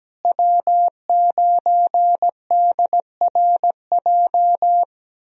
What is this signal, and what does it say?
Morse code, heard as a keyed tone, about 17 wpm, 700 Hz. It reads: W9DRJ
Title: USA Call Signs - 5 Characters - Repeated 3x - 17wpm